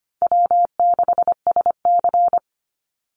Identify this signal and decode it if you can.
Morse code, heard as a keyed tone, about 25 wpm, 700 Hz. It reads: W6SC
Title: Call Signs Repeated 3x - 25wpm